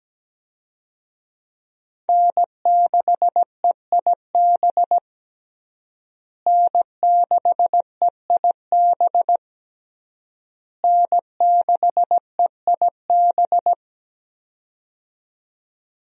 Morse code, heard as a keyed tone, about 17 wpm, 700 Hz. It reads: N6EIB N6EIB N6EIB
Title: USA Call Signs - 5 Characters - Repeated 3x - 17wpm